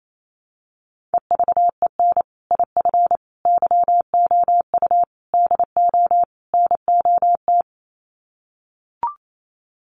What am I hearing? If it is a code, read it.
Morse code, heard as a keyed tone, about 28 wpm, 700 Hz. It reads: EVEN IF YOU DO NOT
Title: Sets of 5 Words Encore 28wpm